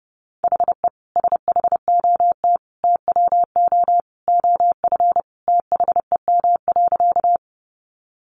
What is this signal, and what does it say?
Morse code, heard as a keyed tone, about 30 wpm, 700 Hz. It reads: HE SHOT TWO OF THEM.
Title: Sentences from Top 1100 Words 30wpm